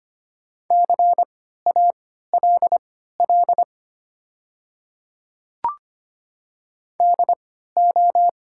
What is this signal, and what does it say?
Morse code, heard as a keyed tone, about 25 wpm, spaced as for 14 wpm, 700 Hz. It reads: CALL DO
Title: 3x Character Spacing - Top 100 Words - 25wpm